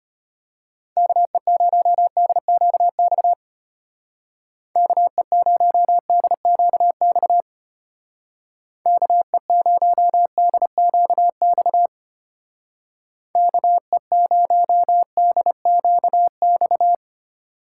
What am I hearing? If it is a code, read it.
Morse code, heard as a keyed tone, about 30 wpm, 700 Hz. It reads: KE0DQX KE0DQX KE0DQX KE0DQX
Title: Speed Racing - USA 2x3 Call Signs 25wpm